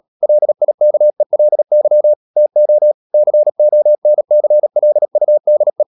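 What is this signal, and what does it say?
Morse code, heard as a keyed tone, about 37 wpm, 600 Hz. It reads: LIKELY TO CONCLUDE
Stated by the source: Morse code practice at 35–39 wpm sending English text of Ken Follett's novel Never, from CW med Gnister.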